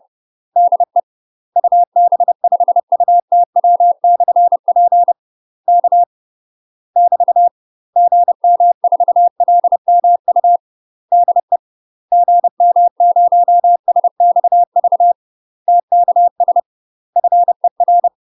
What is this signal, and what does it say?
Morse code, heard as a keyed tone, about 30 wpm, 700 Hz. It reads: DE UB5UTW/P K <BT> GM4LMU DE GM0SXV TKS FER
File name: UK MASTER QSOs TRAINING / 30WPM / UK MASTER QSOs 0013